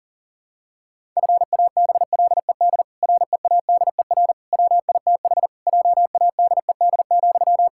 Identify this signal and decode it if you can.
Morse code, heard as a keyed tone, about 40 wpm, 700 Hz. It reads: FABLED READER WITH JADED,
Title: Pangram Challenge v2 - 40wpm